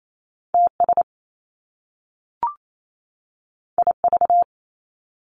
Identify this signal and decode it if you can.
Morse code, heard as a keyed tone, about 28 wpm, 700 Hz. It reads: TS IV
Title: Top Two Letter Combinations in Words 28wpm